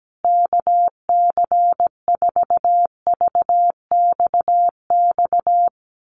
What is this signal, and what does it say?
Morse code, heard as a keyed tone, about 17 wpm, 700 Hz. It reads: KC4VXX
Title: USA Call Signs Repeated 2x 17wpm